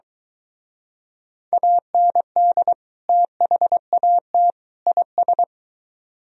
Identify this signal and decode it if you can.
Morse code, heard as a keyed tone, about 24 wpm, 700 Hz. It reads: AND THAT IS